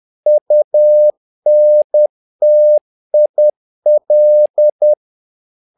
Morse code, heard as a keyed tone, about 10 wpm, 600 Hz. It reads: UNTIL
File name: learning center / tower trouble / tower trouble 10wpm